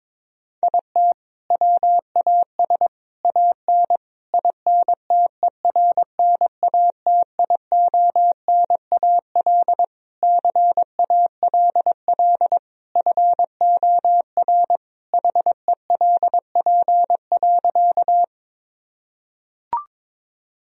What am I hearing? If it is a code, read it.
Morse code, heard as a keyed tone, about 22 wpm, 700 Hz. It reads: IT WAS AN INTERNATIONAL CALL FOR HELP.
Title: Sentences from Top 1000 Words - Review Part 4 - 22wpm